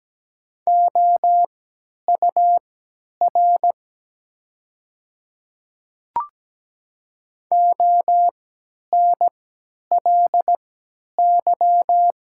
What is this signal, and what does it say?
Morse code, heard as a keyed tone, about 17 wpm, spaced as for 10 wpm, 700 Hz. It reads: OUR ONLY
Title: 3x Character Spacing - Top 200 Words - 17wpm